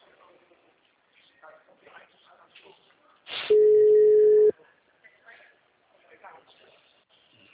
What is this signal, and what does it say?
Ringback tone (Europe, 425 Hz), with speech.